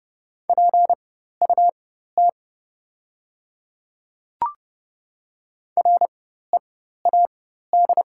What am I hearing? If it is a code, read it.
Morse code, heard as a keyed tone, about 30 wpm, spaced as for 14 wpm, 700 Hz. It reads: PUT READ
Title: 4x Character Spacing - Top 200 Words - 30wpm